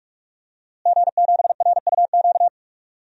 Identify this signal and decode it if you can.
Morse code, heard as a keyed tone, about 46 wpm, 700 Hz. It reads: G7RUQ
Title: DXCC-100 Call Signs - 5 Characters - Repeated 2x - 45wpm